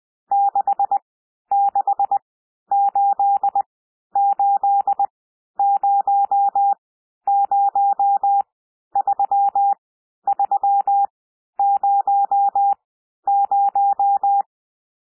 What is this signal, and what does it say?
Morse code, heard as a keyed tone, about 20 wpm, spaced as for 11 wpm, 800 Hz. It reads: 6688003300
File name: cwmp3 / N1 SOUND NUMBERS 20 wpm